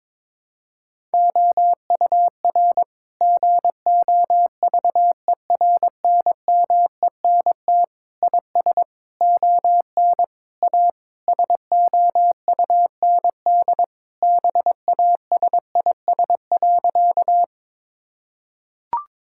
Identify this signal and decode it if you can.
Morse code, heard as a keyed tone, about 22 wpm, 700 Hz. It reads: OUR GOVERNMENT IS ON A SOUND BASIS.
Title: Sentences from Top 1000 Words - Review Part 5 - 22wpm